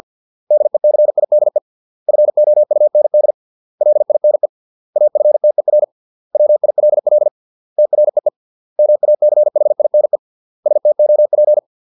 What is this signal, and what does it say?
Morse code, heard as a keyed tone, about 50 wpm, 600 Hz. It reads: DECIDE FOUND LINE AFTER WILL TREE MACHINE STOP
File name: 632 words / 632 Words 50WPM